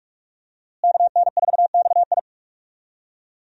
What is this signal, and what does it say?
Morse code, heard as a keyed tone, about 46 wpm, 700 Hz. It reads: KN4XI
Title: USA Call Signs - 5 Characters - Repeated 3x - 45wpm